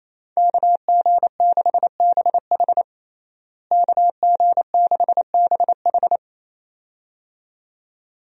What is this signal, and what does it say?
Morse code, heard as a keyed tone, about 28 wpm, 700 Hz. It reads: KG6BH KG6BH
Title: USA Call Signs - 5 Characters - Repeated 2x - 28wpm